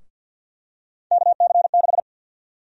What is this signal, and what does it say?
Morse code, heard as a keyed tone, about 51 wpm, 700 Hz. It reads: KC6
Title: USA Call Sign Prefixes - Repeated 2x - 50wpm